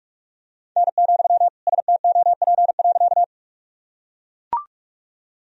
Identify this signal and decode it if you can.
Morse code, heard as a keyed tone, about 41 wpm, 700 Hz. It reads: N, STOP.
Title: Sentences from Top 300 Words and Abbreviations 45wpm